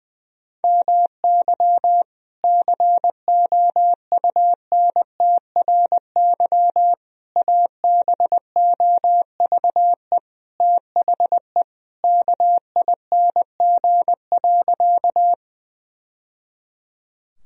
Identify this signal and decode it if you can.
Morse code, heard as a keyed tone, about 21 wpm, 700 Hz. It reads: MY COUNTRY ABOVE THE KING.
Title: Sentences from Top 1000 Words - Review Part 2 - 20wpm